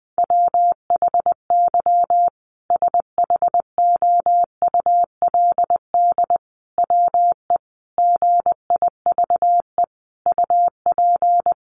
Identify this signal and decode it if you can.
Morse code, heard as a keyed tone, about 20 wpm, 700 Hz. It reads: WHY SHOULD WE GIVE UP
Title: Sentences from Top 700 Words 20wpm